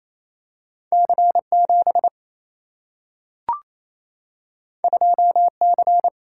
Morse code, heard as a keyed tone, about 28 wpm, 700 Hz. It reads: C7 2C
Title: Single-Letter Single-Number 28wpm